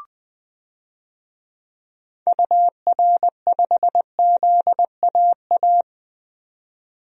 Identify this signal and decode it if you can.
Morse code, heard as a keyed tone, about 20 wpm, 700 Hz. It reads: UR5ZAA
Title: DXCC-100 Call Signs - 6 Characters - 20wpm